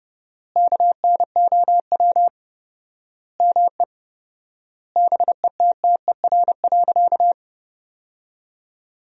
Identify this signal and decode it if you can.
Morse code, heard as a keyed tone, about 32 wpm, 700 Hz. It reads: KNOW ME BETTER.
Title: 4x Word-Spacing - Sentences from Top 400 Words - 30wpm